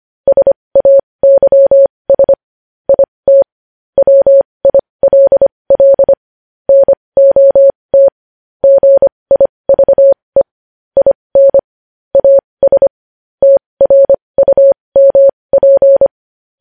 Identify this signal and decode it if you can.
Morse code, heard as a keyed tone, about 25 wpm, spaced as for 20 wpm, 550 Hz. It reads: SAYS IT WILL NOT GIVE IN AS TRUMP
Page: news-headlines-in-morse-code-at-20-wpm